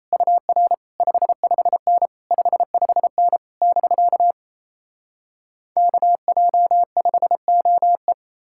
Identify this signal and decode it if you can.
Morse code, heard as a keyed tone, about 38 wpm, 700 Hz. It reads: UR 55N 55N <BK> KJ5OE